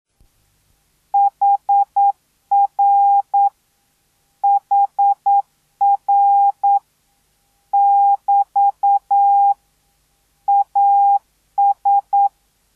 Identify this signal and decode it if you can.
Morse code, heard as a keyed tone, about 9 wpm, 800 Hz. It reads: HR HR <BT> AS